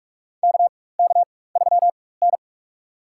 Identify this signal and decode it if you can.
Morse code, heard as a keyed tone, about 46 wpm, spaced as for 21 wpm, 700 Hz. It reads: KK3N
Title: USA Call Signs - 4 Characters - 4x Character Spacing - 45wpm